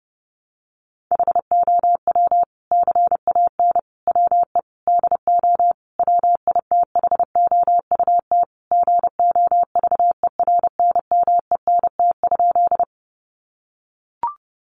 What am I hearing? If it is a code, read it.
Morse code, heard as a keyed tone, about 30 wpm, 700 Hz. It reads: HOW CAN WE DO WITHOUT GOVERNMENT?
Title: Sentences from Top 600 Words 30wpm